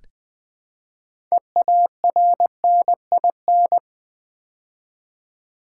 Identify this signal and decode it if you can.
Morse code, heard as a keyed tone, about 20 wpm, 700 Hz. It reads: EARNIN
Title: Build A Word - 2 Letters At A Time - 8-Letter Words - 20wpm